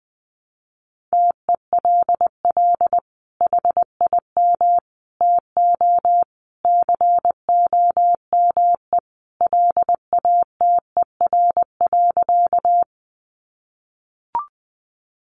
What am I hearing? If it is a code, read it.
Morse code, heard as a keyed tone, about 20 wpm, 700 Hz. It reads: TELL HIM TO COME LATER.
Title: Sentences from Top 700 Words 20wpm